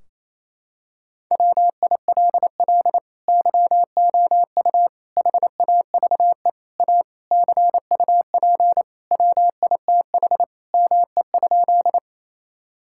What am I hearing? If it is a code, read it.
Morse code, heard as a keyed tone, about 28 wpm, 700 Hz. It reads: WILL YOU HAVE A CUP WITH ME?
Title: Sentences from Top 1600 Words 28wpm